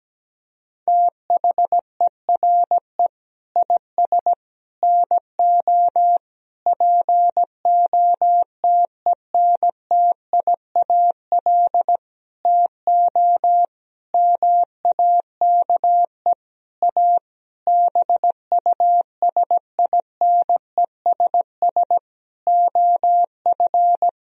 Morse code, heard as a keyed tone, about 17 wpm, 700 Hz. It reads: THERE IS NO POTENTIAL TO MAKE A BUSINESS OF
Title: Sentences from Top 1000 Words - Review Part 4 - 17wpm